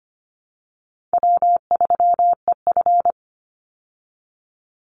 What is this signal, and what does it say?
Morse code, heard as a keyed tone, about 25 wpm, 700 Hz. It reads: W3EF
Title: Easy Call Signs Repeated 2x - 25wpm